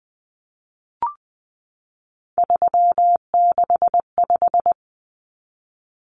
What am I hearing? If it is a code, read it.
Morse code, heard as a keyed tone, about 20 wpm, 700 Hz. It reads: 365
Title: Common Cut Numbers - 3-Digit 20wpm